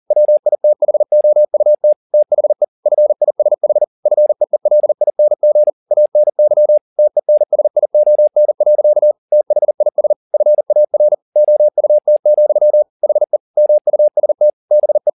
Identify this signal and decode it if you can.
Morse code, heard as a keyed tone, about 40 wpm, 600 Hz. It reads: WITHOUT THE FISH FEELING ANY TENSION. THIS FAR OUT, HE MUST BE